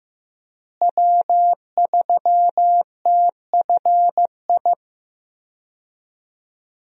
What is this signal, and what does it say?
Morse code, heard as a keyed tone, about 15 wpm, 700 Hz. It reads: W3TFI
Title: USA Call Signs - 5 Characters - Repeated 3x - 15wpm